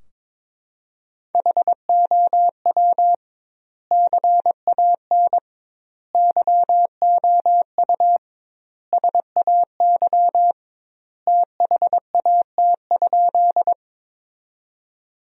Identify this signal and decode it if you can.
Morse code, heard as a keyed tone, about 22 wpm, 700 Hz. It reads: HOW CAN YOU SAY THAT?
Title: 2x Word-Spacing - Sentences from Top 200 Words - 22wpm